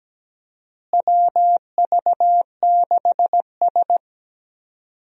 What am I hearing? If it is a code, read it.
Morse code, heard as a keyed tone, about 17 wpm, 700 Hz. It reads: WV6S
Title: USA Call Signs - 4 Characters - Repeated 3x - 17wpm